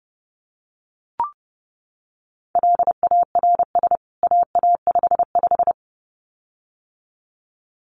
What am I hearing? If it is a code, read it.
Morse code, heard as a keyed tone, about 30 wpm, 700 Hz. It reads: LARS AA55
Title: CWT Exchange 30wpm